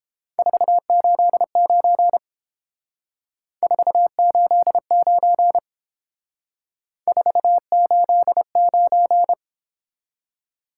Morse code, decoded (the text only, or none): 489 489 489